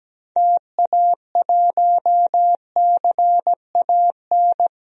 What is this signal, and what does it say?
Morse code, heard as a keyed tone, about 17 wpm, 700 Hz. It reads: TA1CAN
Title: DXCC-100 Call Signs - 6 Characters - 17wpm